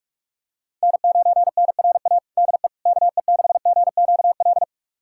Morse code, heard as a keyed tone, about 46 wpm, 700 Hz. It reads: N9NRA DE KE6GQL
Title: USA Callsign DE Callsign 45wpm